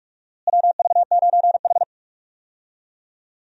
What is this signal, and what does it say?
Morse code, heard as a keyed tone, about 45 wpm, 700 Hz. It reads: WV9H
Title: USA Call Signs - 4 Characters - Repeated 3x - 45wpm